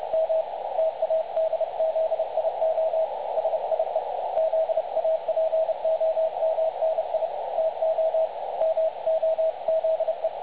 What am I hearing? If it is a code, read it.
Morse code, heard as a keyed tone, about 30 wpm, 650 Hz. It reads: M TADZIO SIE ZAPOMNI TO MOZE
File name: dlazaawansowanych / nagraniazpasma / websdr recording 3565.0kHz